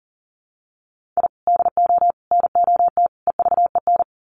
Morse code, heard as a keyed tone, about 40 wpm, 700 Hz. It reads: I DO NOT EVEN
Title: Sets of 4 Words 40wpm